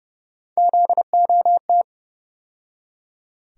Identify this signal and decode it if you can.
Morse code, heard as a keyed tone, about 30 wpm, 700 Hz. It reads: ZOT
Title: USA Call Sign Postfixes - 3 Letters 30wpm